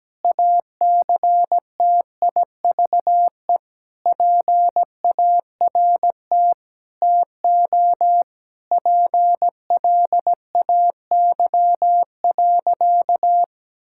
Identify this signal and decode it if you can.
Morse code, heard as a keyed tone, about 17 wpm, 700 Hz. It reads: ACTIVE PART TO PLAY.
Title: Sentences from Top 1000 Words - Review Part 6 - 17wpm